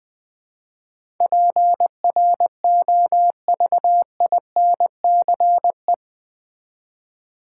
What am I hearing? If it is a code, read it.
Morse code, heard as a keyed tone, about 20 wpm, 700 Hz. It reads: PROVINCE
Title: Build A Word - 4 Letters At A Time - 8-Letter Words - 20wpm